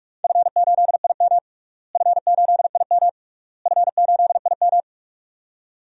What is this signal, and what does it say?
Morse code, heard as a keyed tone, about 41 wpm, 700 Hz. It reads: F8IM F8IM F8IM
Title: DXCC-100 Call Signs - 4 Characters - Repeated 3x - 45wpm